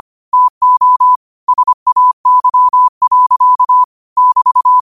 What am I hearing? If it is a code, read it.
Morse code, heard as a keyed tone, about 25 wpm, 1000 Hz. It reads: TO SAY. <BT>
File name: qotdfiles / 25wpm